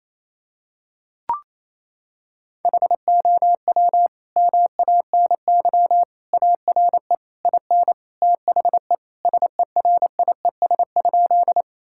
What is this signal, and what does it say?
Morse code, heard as a keyed tone, about 28 wpm, 700 Hz. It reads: HOW MANY ARE IN THE SERIES?
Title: Sentences from Top 1000 Words - Review Part 5 - 28wpm